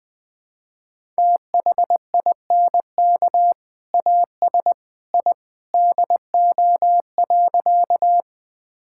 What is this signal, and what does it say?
Morse code, heard as a keyed tone, about 20 wpm, 700 Hz. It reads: THINK AS I DO.
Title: Sentences with 2-Word Phrases - 20wpm